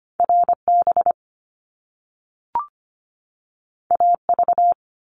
Morse code, heard as a keyed tone, about 25 wpm, 700 Hz. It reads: RB AV
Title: Top Two Letter Combinations in Words 25wpm